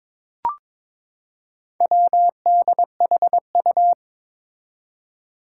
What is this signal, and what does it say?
Morse code, heard as a keyed tone, about 21 wpm, 700 Hz. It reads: WDHU
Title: Four Challenging Letter-Number Combinations 22wpm